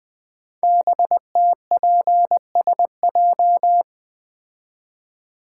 Morse code, heard as a keyed tone, about 20 wpm, 700 Hz. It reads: BTPSJ
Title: Code Groups - 5 Letters - 20wpm